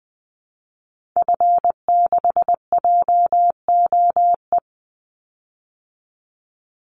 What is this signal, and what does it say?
Morse code, heard as a keyed tone, about 20 wpm, 700 Hz. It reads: F6JOE
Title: Call Signs - 20wpm